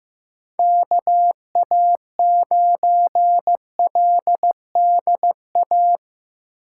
Morse code, heard as a keyed tone, about 15 wpm, 700 Hz. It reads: KA9LDA
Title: USA Call Signs - 6 Characters - 8x Character Spacing - 15wpm